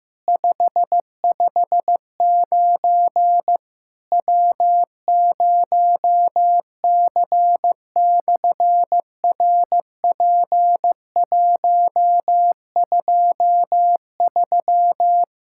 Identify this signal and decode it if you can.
Morse code, heard as a keyed tone, about 15 wpm, 700 Hz. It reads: 559 W0C/RP123
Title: SOTA QSO Practice 15wpm